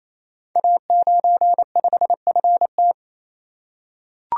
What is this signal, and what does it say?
Morse code, heard as a keyed tone, about 29 wpm, 700 Hz. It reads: A95FT
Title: Code Groups - 5 Letter:Numbers - 28wpm